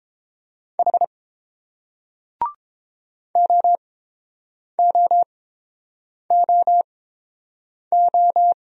Morse code, heard as a keyed tone, about 30 wpm, spaced as for 7 wpm, 700 Hz. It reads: H OOOO